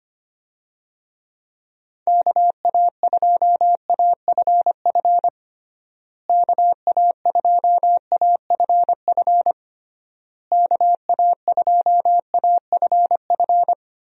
Morse code, heard as a keyed tone, about 25 wpm, 700 Hz. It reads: KA2AFF KA2AFF KA2AFF
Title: USA Call Signs - 6 Characters - Repeated 3x - 25wpm